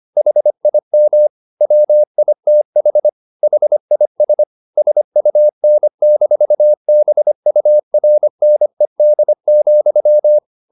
Morse code, heard as a keyed tone, about 25 wpm, 600 Hz. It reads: HIM WITH HIS SUN-BURNED,
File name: the old man and the sea part 1 25wpm